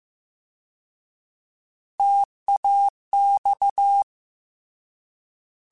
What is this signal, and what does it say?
Morse code, heard as a keyed tone, about 15 wpm, 800 Hz. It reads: TAX
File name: Alphabetical / Alphabetical mp3s / Ts @13